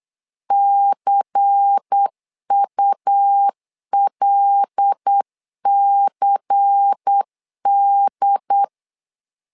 Morse code, heard as a keyed tone, about 8 wpm, 800 Hz. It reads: CULCD
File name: LEZIONI / 5 LBVC / 5 LBVC 8wpm 2935